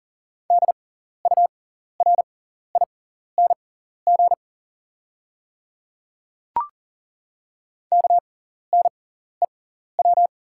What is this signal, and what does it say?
Morse code, heard as a keyed tone, about 40 wpm, spaced as for 14 wpm, 700 Hz. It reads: DURING KNEW
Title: 6x Character Spacing - Top 500 Words - 40wpm